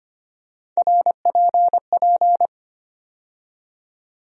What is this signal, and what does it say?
Morse code, heard as a keyed tone, about 25 wpm, 700 Hz. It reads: RPP
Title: USA Call Sign Postfixes - 3 Letters 25wpm